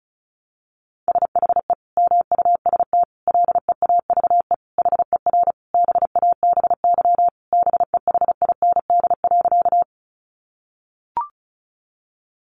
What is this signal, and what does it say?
Morse code, heard as a keyed tone, about 34 wpm, 700 Hz. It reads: SHE MUST LEAVE HER BABY BEHIND.
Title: Sentences from Top 1300 Words 35wpm